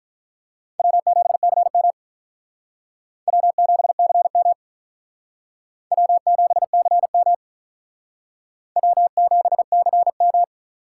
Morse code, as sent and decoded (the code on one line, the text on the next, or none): .-- --... -.-. -- / .-- --... -.-. -- / .-- --... -.-. -- / .-- --... -.-. --
W7CM W7CM W7CM W7CM